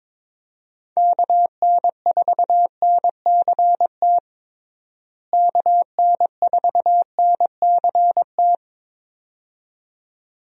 Morse code, heard as a keyed tone, about 21 wpm, 700 Hz. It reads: KN4NCT KN4NCT
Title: USA Call Signs - 6 Characters - Repeated 2x - 22wpm